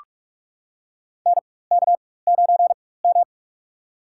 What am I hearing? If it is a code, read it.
Morse code, heard as a keyed tone, about 42 wpm, spaced as for 20 wpm, 700 Hz. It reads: NK9M
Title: USA Call Signs - 4 Characters - 4x Character Spacing - 45wpm